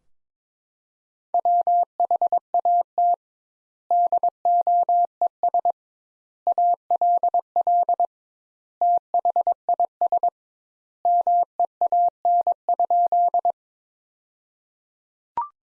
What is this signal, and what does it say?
Morse code, heard as a keyed tone, about 22 wpm, 700 Hz. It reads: WHAT DOES ALL THIS MEAN?
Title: Sentences Top 200 Words - 22wpm